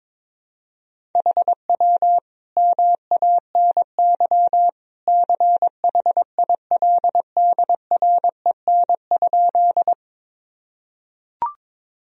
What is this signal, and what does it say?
Morse code, heard as a keyed tone, about 21 wpm, 700 Hz. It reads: HW MANY CHILDREN?
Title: Sentences from Top 300 Words and Abbreviations 22wpm